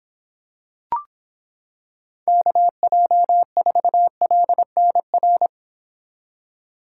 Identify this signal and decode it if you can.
Morse code, heard as a keyed tone, about 26 wpm, 700 Hz. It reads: KJ4LNR